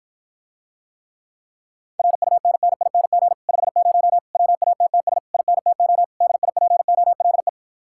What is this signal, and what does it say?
Morse code, heard as a keyed tone, about 54 wpm, 700 Hz. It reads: RUNNING 50 WATTS INTO DIPOLE